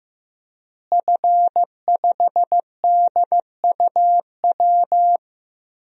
Morse code, heard as a keyed tone, about 16 wpm, 700 Hz. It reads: F5DUW
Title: DXCC-100 Call Signs - 5 Characters - Repeated 2x - 15wpm